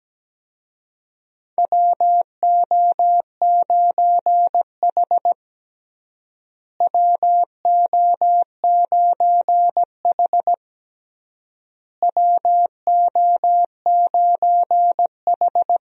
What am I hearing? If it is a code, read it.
Morse code, heard as a keyed tone, about 17 wpm, 700 Hz. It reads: WO9H WO9H WO9H
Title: USA Call Signs - 4 Characters - Repeated 3x - 17wpm